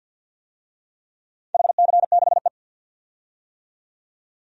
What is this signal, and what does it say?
Morse code, heard as a keyed tone, about 46 wpm, 700 Hz. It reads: HC6E
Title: DXCC-100 Call Signs - 4 Characters - 50wpm